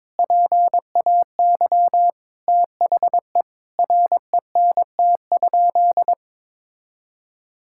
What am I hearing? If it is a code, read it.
Morse code, heard as a keyed tone, about 22 wpm, 700 Hz. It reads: PAY THE RENT?
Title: Sentences from Top 1000 Words - Review Part 5 - 22wpm